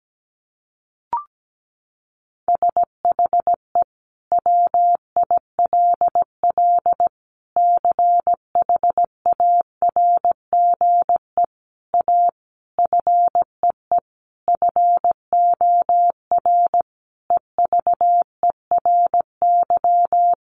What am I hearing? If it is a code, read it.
Morse code, heard as a keyed tone, about 17 wpm, 700 Hz. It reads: SHE WILL CHARGE A FEE FOR EVERY
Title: Sentences from Top 1500 Words 17wpm